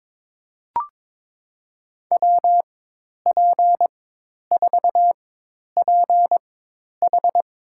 Morse code, heard as a keyed tone, about 21 wpm, spaced as for 10 wpm, 700 Hz. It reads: WP4PH
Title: USA Call Signs - 4x Character Spacing - 22wpm